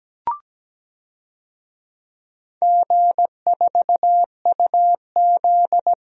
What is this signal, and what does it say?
Morse code, heard as a keyed tone, about 17 wpm, 700 Hz. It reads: G4UZ
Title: DXCC-100 Call Signs - 4 Characters - Repeated 2x - 17wpm